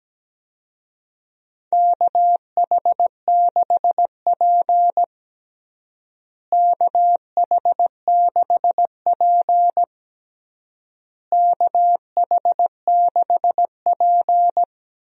Morse code, heard as a keyed tone, about 17 wpm, 700 Hz. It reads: KH6P KH6P KH6P
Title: DXCC-100 Call Signs - 4 Characters - Repeated 3x - 17wpm